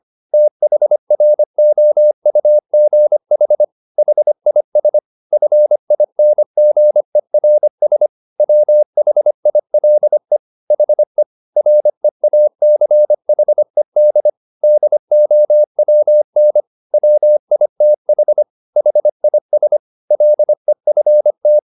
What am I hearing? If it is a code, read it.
Morse code, heard as a keyed tone, about 25 wpm, 600 Hz. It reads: THROUGH HIS FINGERS WHILE HE REACHED DOWN WITH HIS LEFT
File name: the old man and the sea part 1 25wpm